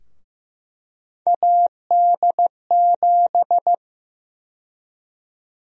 Morse code, heard as a keyed tone, about 15 wpm, 700 Hz. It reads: AD7
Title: USA Call Sign Prefixes - Repeated 2x - 15wpm